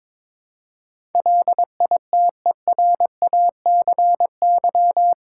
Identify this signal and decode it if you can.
Morse code, heard as a keyed tone, about 22 wpm, 700 Hz. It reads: LITERACY
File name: practice / 22wpm / Eight Letter Words - 22wpm